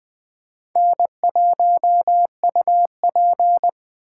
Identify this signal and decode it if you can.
Morse code, heard as a keyed tone, about 20 wpm, 700 Hz. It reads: N1UP
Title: USA Call Signs - 4 Characters - 3x Character Spacing - 20wpm